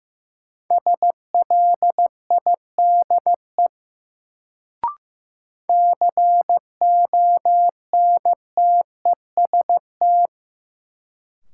Morse code, heard as a keyed tone, about 15 wpm, 700 Hz. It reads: SLIDE CONTEST
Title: Sentences from Top 1600 Words - New Word Drill - 15wpm